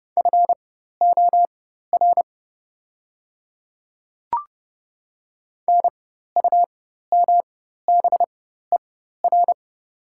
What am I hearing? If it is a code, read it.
Morse code, heard as a keyed tone, about 30 wpm, spaced as for 14 wpm, 700 Hz. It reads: FOR NUMBER
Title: 4x Character Spacing - Top 200 Words - 30wpm